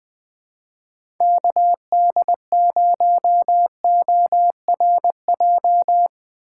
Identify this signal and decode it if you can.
Morse code, heard as a keyed tone, about 20 wpm, 700 Hz. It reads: KD0ORJ
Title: USA Call Signs - 3x Character Spacing - 20wpm